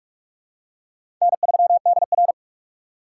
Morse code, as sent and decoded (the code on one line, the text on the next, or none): -. ...-- -.. .-.
N3DR